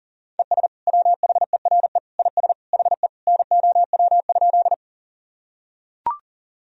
Morse code, heard as a keyed tone, about 37 wpm, 700 Hz. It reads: ES WHERE IS HE NOW?
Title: Sentences from Top 200 Words and Abbreviations 40wpm